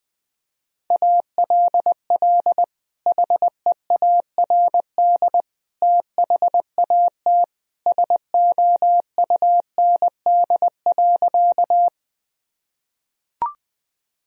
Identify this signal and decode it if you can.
Morse code, heard as a keyed tone, about 20 wpm, 700 Hz. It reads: ALL HEARD THAT SOUND.
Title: Sentences from Top 1000 Words - Review Part 1 - 20wpm